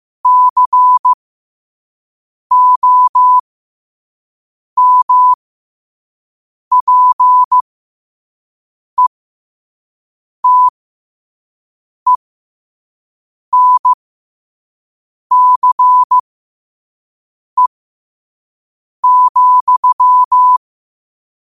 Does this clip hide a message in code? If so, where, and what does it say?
Yes: Morse code from 0.3 to 20.6 s, heard as a keyed tone, about 15 wpm, spaced as for 5 wpm, 1000 Hz. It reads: COMPETENCE,